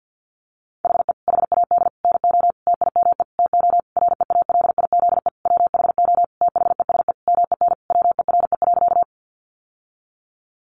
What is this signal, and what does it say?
Morse code, heard as a keyed tone, about 48 wpm, 700 Hz. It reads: HE HAD NO TIME TO REALIZE WHO THESE MEN WERE.
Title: Sentences from Top 1000 Words 50wpm